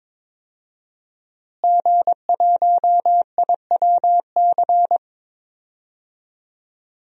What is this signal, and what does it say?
Morse code, heard as a keyed tone, about 22 wpm, 700 Hz. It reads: G1IWC
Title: DXCC-100 Call Signs - 5 Characters - Repeated 3x - 22wpm